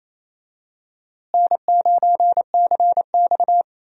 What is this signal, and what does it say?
Morse code, heard as a keyed tone, about 28 wpm, 700 Hz. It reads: N9CX
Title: USA Call Signs - 4 Characters - Repeated 2x - 28wpm